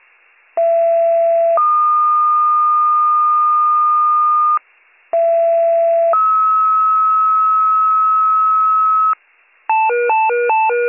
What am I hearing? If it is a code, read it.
Two-tone page, heard as a one-second tone, then a three-second tone, about 670 then 1220 Hz, about 670 then 1290 Hz. It is heard from 0.6 to 9.1 s.